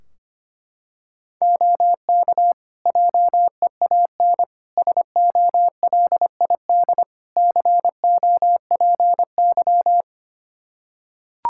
Morse code, heard as a keyed tone, about 25 wpm, 700 Hz. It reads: OK JEAN SOLID COPY
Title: Practice QSO - Exchange 3 - More Info - 25wpm